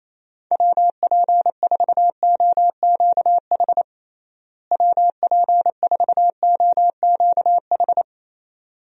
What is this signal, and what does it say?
Morse code, heard as a keyed tone, about 28 wpm, 700 Hz. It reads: WP4OQH WP4OQH
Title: USA Call Signs - 6 Characters - Repeated 2x - 28wpm